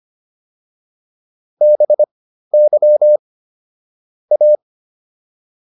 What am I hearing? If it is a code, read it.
Morse code, heard as a keyed tone, about 25 wpm, spaced as for 13 wpm, 600 Hz. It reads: BY A